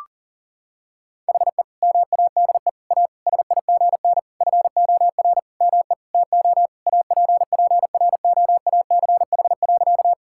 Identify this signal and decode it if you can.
Morse code, heard as a keyed tone, about 37 wpm, 700 Hz. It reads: HE MADE A SIGN FOR ME TO APPROACH.
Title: Sentences from Top 1000 Words - Review Part 6 - 40wpm